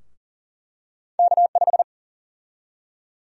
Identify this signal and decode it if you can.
Morse code, heard as a keyed tone, about 40 wpm, 700 Hz. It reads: K5
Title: USA Call Sign Prefixes - Repeated 2x - 40wpm